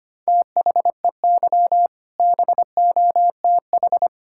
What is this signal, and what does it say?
Morse code, heard as a keyed tone, about 25 wpm, 700 Hz. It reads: THEY BOTH
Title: Sentences from Top 1600 Words 25wpm